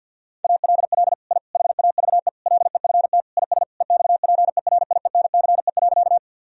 Morse code, heard as a keyed tone, about 50 wpm, 700 Hz. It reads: ALL I HAVE LEFT IS EXPERIENCE.
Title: Sentences from Top 1000 Words - Review Part 3 - 50wpm